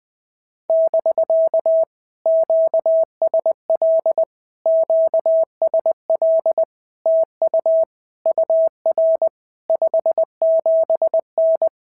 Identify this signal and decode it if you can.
Morse code, heard as a keyed tone, about 20 wpm, 650 Hz. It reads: <BK> QSL QSL TU UR 57N
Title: POTA QSO Practice - 20wpm